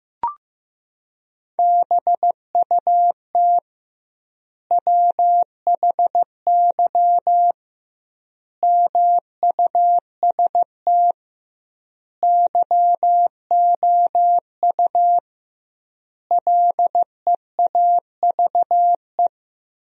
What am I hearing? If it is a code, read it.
Morse code, heard as a keyed tone, about 15 wpm, 700 Hz. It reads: BUT WHY MUST YOU LEAVE
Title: 2x Word-Spacing - Sentences from Top 400 Words - 15wpm